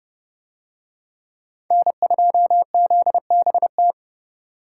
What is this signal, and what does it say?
Morse code, heard as a keyed tone, about 29 wpm, 700 Hz. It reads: N2ZBT